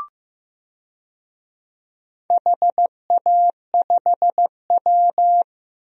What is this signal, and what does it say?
Morse code, heard as a keyed tone, about 15 wpm, 700 Hz. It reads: HA5W
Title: DXCC-100 Call Signs - 4 Characters - 15wpm